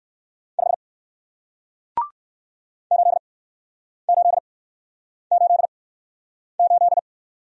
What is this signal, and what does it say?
Morse code, heard as a keyed tone, about 61 wpm, spaced as for 9 wpm, 700 Hz. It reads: 5 8888